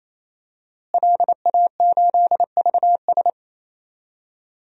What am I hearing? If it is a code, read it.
Morse code, heard as a keyed tone, about 28 wpm, 700 Hz. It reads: LA8VS
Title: DXCC-100 Call Signs - 5 Characters - Repeated 3x - 28wpm